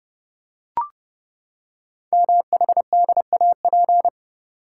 Morse code, heard as a keyed tone, about 30 wpm, 700 Hz. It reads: MHDAP
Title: Code Groups - 5 Letters - 30wpm